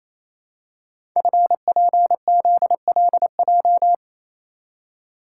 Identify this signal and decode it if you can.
Morse code, heard as a keyed tone, about 29 wpm, 700 Hz. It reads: FPZLJ
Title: Code Groups - 5 Letter:Numbers - 28wpm